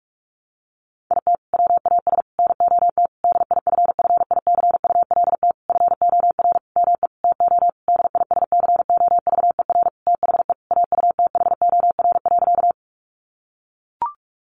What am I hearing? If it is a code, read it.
Morse code, heard as a keyed tone, about 44 wpm, 700 Hz. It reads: IT WAS NOT DIFFICULT FOR ME TO DISCOVER THE AUTHOR.
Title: Sentences from Top 1200 Words 45wpm